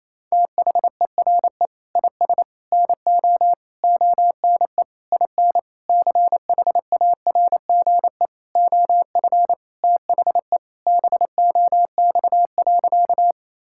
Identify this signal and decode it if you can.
Morse code, heard as a keyed tone, about 28 wpm, 700 Hz. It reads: THERE IS NO ONE IN CHARGE OF THE BOX.
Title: Sentences from Top 1000 Words - Review Part 6 - 28wpm